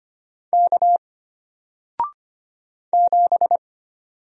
Morse code, heard as a keyed tone, about 25 wpm, 700 Hz. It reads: K 7
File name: Single-Letter-Number-25wpm